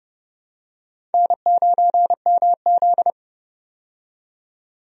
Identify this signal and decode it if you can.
Morse code, heard as a keyed tone, about 31 wpm, 700 Hz. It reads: N9MZ
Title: USA Call Signs - 4 Characters - 30wpm